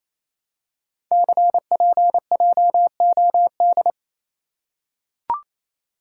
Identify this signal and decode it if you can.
Morse code, heard as a keyed tone, about 28 wpm, 700 Hz. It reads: CPJOD
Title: Code Groups - 5 Letters - 28wpm